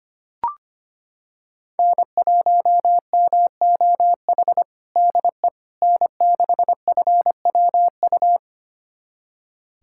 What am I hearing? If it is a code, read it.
Morse code, heard as a keyed tone, about 25 wpm, 700 Hz. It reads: N1MOH DE N6FWU
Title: USA Callsign DE Callsign - 5 Characters - 25wpm